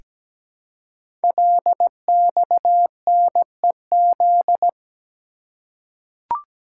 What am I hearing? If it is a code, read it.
Morse code, heard as a keyed tone, about 17 wpm, 700 Hz. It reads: LXNEZ
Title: Code Groups - 5 Letters - 17wpm